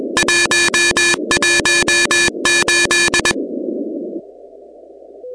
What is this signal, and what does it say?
Morse code, heard as a keyed tone, about 21 wpm, 1750 Hz, with noise. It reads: 118
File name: contests / 2019-cw-wpx-kt5j-k5pi- -2330 0-16